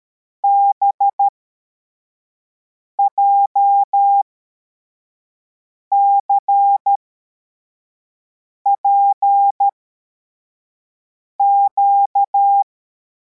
Morse code, heard as a keyed tone, about 13 wpm, spaced as for 4 wpm, 800 Hz. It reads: BJCPQ
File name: formation / cw / Text81